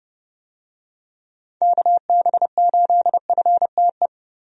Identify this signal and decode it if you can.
Morse code, heard as a keyed tone, about 30 wpm, 700 Hz. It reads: KB8FTE
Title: USA Call Signs - 6 Characters - 30wpm